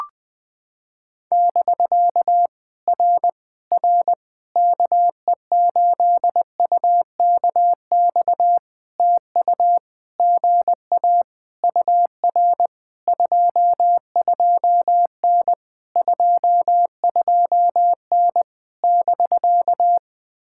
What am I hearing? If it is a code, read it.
Morse code, heard as a keyed tone, about 20 wpm, 700 Hz. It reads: <BK> R R KE8UKX TU GA UR 22N 22N <BK>
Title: POTA QSO Practice - 20wpm